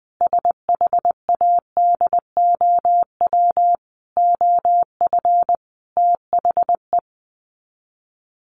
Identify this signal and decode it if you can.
Morse code, heard as a keyed tone, about 20 wpm, 700 Hz. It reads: SHADOW OF THE
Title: Sets of 5 Words Encore 20wpm